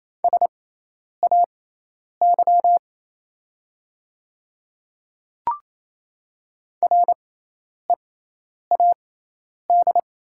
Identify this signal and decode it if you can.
Morse code, heard as a keyed tone, about 28 wpm, spaced as for 10 wpm, 700 Hz. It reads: SAY READ
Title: 6x Character Spacing - Top 200 Words - 28wpm